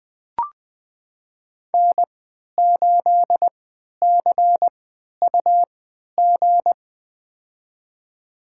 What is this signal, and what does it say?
Morse code, heard as a keyed tone, about 21 wpm, spaced as for 12 wpm, 700 Hz. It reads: N8CUG
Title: USA Call Signs - 5 Characters - 3x Character Spacing - 20wpm